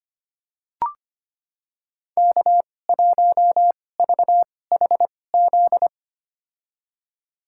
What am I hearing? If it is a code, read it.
Morse code, heard as a keyed tone, about 25 wpm, spaced as for 18 wpm, 700 Hz. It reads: K1VHZ
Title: USA Call Signs - 5 Characters - 2x Character Spacing - 25wpm